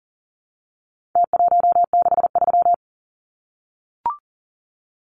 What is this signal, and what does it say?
Morse code, heard as a keyed tone, about 40 wpm, 700 Hz. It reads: T163
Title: Numbers - Contest Style - 4-Digit 40wpm